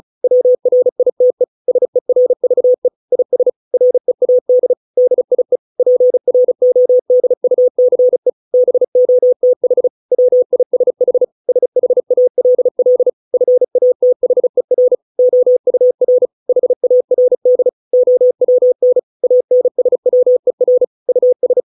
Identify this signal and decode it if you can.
Morse code, heard as a keyed tone, about 34 wpm, 500 Hz. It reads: WRITE SERVE IS READ DIE PRODUCE BOTH WISH SHALL FATHER OUR HARD OWN ANSWER US